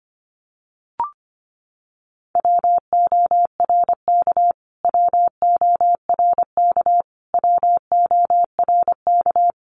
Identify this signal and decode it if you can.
Morse code, heard as a keyed tone, about 24 wpm, 700 Hz. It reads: WORK WORK WORK
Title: Top 500 Words Repeated 25wpm